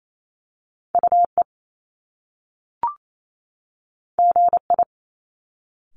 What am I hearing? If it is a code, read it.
Morse code, heard as a keyed tone, about 28 wpm, 700 Hz. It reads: UE GI
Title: Top Two Letter Combinations in Words 28wpm